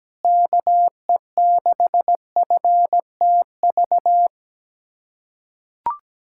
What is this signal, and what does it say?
Morse code, heard as a keyed tone, about 17 wpm, 700 Hz. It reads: KE6FTV
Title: USA Callsign DE Callsign 17wpm